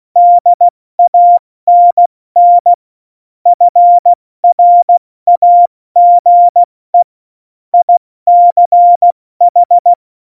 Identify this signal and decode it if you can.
Morse code, heard as a keyed tone, about 16 wpm, spaced as for 14 wpm, 700 Hz. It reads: DANN FRAGE ICH
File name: Morsekurs / Witze / Witz 16. April 2019, 70 F80